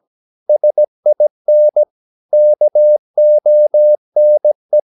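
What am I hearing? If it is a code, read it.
Morse code, heard as a keyed tone, about 17 wpm, 600 Hz. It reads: SIN KONE
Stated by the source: Slow (QRS) Morse practice at 17 wpm, from CW med Gnister.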